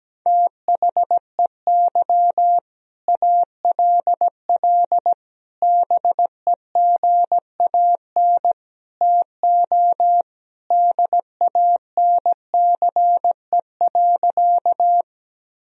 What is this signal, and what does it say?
Morse code, heard as a keyed tone, about 17 wpm, 700 Hz. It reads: THEY ALL BEGAN TO DANCE.
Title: Sentences from Top 1000 Words - Review Part 4 - 17wpm